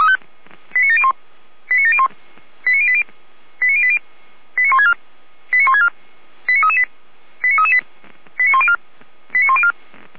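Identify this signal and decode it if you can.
Five-tone selective call (ZVEI), heard as quick runs of five tones, tones of about 70 ms. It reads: 89701 89701 89080 89080 89164 89164 89338 89338 89104 89104
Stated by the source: ZVEI five-tone selective call received by SDR.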